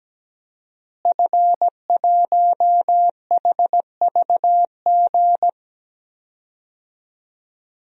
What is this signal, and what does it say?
Morse code, heard as a keyed tone, about 17 wpm, 700 Hz. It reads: F1HVG
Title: DXCC-100 Call Signs - 5 Characters - Repeated 2x - 17wpm